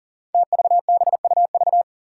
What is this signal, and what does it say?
Morse code, heard as a keyed tone, about 39 wpm, 700 Hz. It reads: TVBUV
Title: Five Challenging Letter-Number Combinations 40wpm